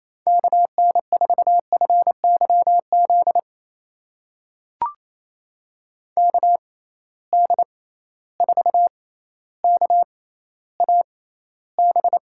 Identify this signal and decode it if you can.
Morse code, heard as a keyed tone, about 28 wpm, spaced as for 14 wpm, 700 Hz. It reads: KN4FYZ KD4KAB
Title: USA Call Signs - 6 Characters - 6x Character Spacing - 28wpm